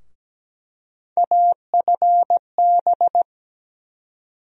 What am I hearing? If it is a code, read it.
Morse code, heard as a keyed tone, about 17 wpm, 700 Hz. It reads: AFB
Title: USA Call Sign Postfixes - 3 Letters 17wpm